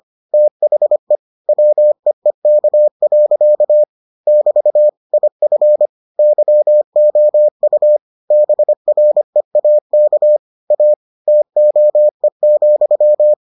Morse code, heard as a keyed tone, about 25 wpm, 600 Hz. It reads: THE WEEK. <BT> IF YOU BREAK A TOE,